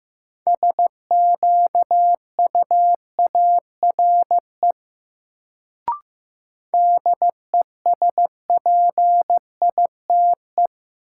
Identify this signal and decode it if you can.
Morse code, heard as a keyed tone, about 15 wpm, 700 Hz. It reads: SQUARE DESPITE
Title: Sentences from Top 1000 Words - Review Part 7 - New Word Drill - 15wpm